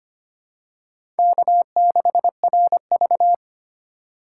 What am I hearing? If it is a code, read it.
Morse code, heard as a keyed tone, about 25 wpm, 700 Hz. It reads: K6RV
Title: USA Call Signs - 4 Characters - 2x Character Spacing - 25wpm